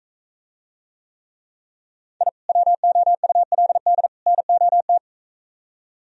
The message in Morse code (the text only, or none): I WOULD NOT